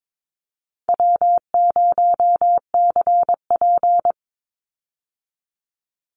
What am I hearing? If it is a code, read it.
Morse code, heard as a keyed tone, about 22 wpm, 700 Hz. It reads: W0CP
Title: Call Signs - 22wpm